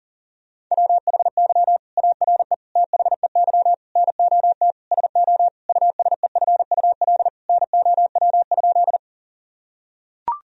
Morse code, heard as a keyed tone, about 40 wpm, 700 Hz. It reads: WHY ARE THEY NOT SO USEFUL NOW?
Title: Sentences from Top 1000 Words - Review Part 4 - 40wpm